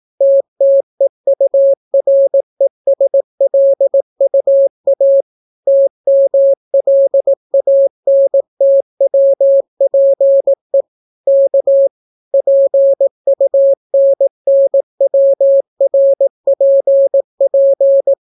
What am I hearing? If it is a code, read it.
Morse code, heard as a keyed tone, about 18 wpm, 550 Hz. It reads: TTEURESLUA TMLANTWPE K PUNNWRPP